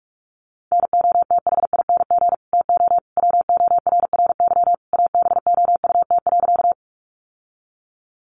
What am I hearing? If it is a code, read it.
Morse code, heard as a keyed tone, about 44 wpm, 700 Hz. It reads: NOTHING TO WORRY ABOUT.
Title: Sentences from Top 1100 Words 45wpm